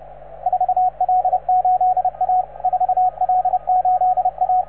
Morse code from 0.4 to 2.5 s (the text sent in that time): 4L8A